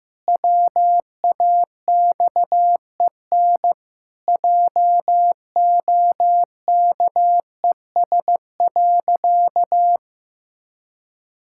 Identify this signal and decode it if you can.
Morse code, heard as a keyed tone, about 16 wpm, 700 Hz. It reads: WAXEN JOKES.
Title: Pangram Challenge v2 - 15wpm